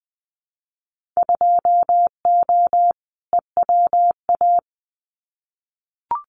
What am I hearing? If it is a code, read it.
Morse code, heard as a keyed tone, about 20 wpm, 700 Hz. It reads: 2O EWA
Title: Winter Field Day Exchanges Encore 20wpm